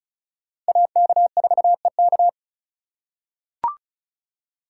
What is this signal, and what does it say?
Morse code, heard as a keyed tone, about 36 wpm, 700 Hz. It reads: AK4EK
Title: USA Call Signs - 3x Character Spacing - 35wpm